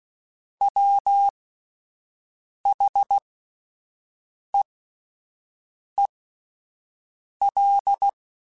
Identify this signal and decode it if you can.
Morse code, heard as a keyed tone, about 16 wpm, spaced as for 5 wpm, 800 Hz. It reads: WHEEL
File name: Spaced / Chuckles Spaced-mp3s / Squeaky sp @10